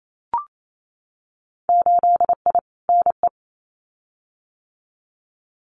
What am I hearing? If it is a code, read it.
Morse code, heard as a keyed tone, about 28 wpm, 700 Hz. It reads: 8I NE
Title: Winter Field Day Exchanges Encore 28wpm